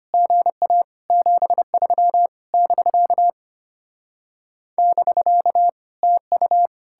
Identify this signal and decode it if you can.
Morse code, heard as a keyed tone, about 33 wpm, 700 Hz. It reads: GA 73 <BK> <BK> TU